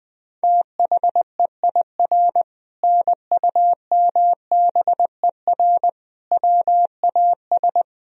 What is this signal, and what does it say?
Morse code, heard as a keyed tone, about 20 wpm, 700 Hz. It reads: THEIR NUMBER WAS
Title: Sentences from Top 1000 Words - Review Part 1 - 20wpm